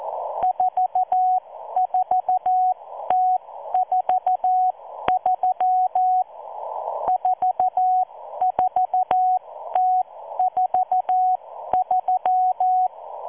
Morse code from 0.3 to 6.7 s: 44T43